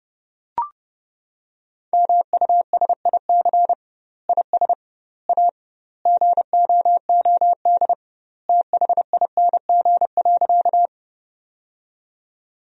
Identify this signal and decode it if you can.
Morse code, heard as a keyed tone, about 30 wpm, 700 Hz. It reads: MUSIC IS A GOOD THING.
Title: 2x Word-Spacing - Sentences from Top 300 Words - 30wpm